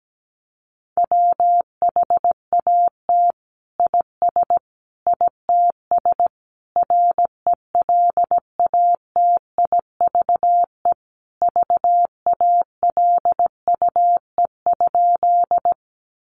Morse code, heard as a keyed tone, about 17 wpm, 700 Hz. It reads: WHAT IS ITS RELATIVE VALUE?
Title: Sentences from Top 1400 Words 17wpm